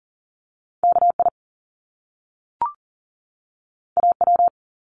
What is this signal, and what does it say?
Morse code, heard as a keyed tone, about 40 wpm, 700 Hz. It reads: KI AW
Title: Top Two Letter Combinations in Words 40wpm